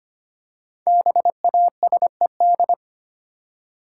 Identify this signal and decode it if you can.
Morse code, heard as a keyed tone, about 25 wpm, 700 Hz. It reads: BASED
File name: 5 Letter Words - First 3 Letters Given - 25wpm